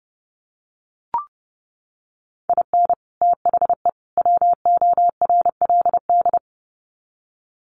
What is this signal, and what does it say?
Morse code, heard as a keyed tone, about 30 wpm, 700 Hz. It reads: IN THE WORLD
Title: Sets of 3 Words 30wpm